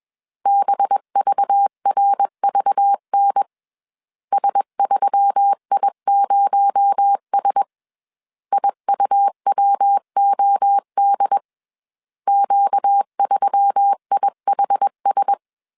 Morse code, heard as a keyed tone, about 21 wpm, 800 Hz. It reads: BVRVN S3I0S IUWOD Q3IHS
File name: CW / DET / 1 6 1500C S A 20wpm 2048